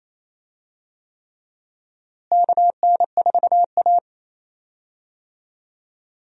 Morse code, heard as a keyed tone, about 28 wpm, 700 Hz. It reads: KN4A
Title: USA Call Signs - 4 Characters - Repeated 3x - 28wpm